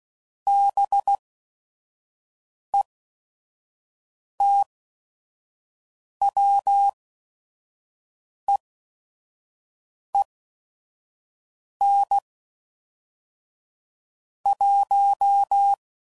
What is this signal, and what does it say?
Morse code, heard as a keyed tone, about 16 wpm, spaced as for 5 wpm, 800 Hz. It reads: BETWEEN 1
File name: CW Code Groups @5WPM / Weather Sets / Weather 4 28